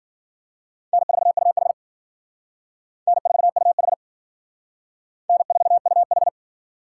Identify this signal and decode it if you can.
Morse code, heard as a keyed tone, about 51 wpm, 700 Hz. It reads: N4UH N4UH N4UH